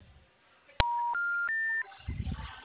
Special information tone.